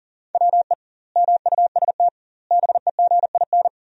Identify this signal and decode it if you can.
Morse code, heard as a keyed tone, about 40 wpm, 700 Hz. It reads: WE MUST BEGIN
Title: 2x Word-Spacing - Sentences from Top 300 Words - 40wpm